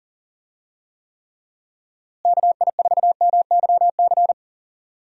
Morse code, heard as a keyed tone, about 37 wpm, 700 Hz. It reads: KI4MYC
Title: USA Call Signs - 6 Characters - Repeated 3x - 40wpm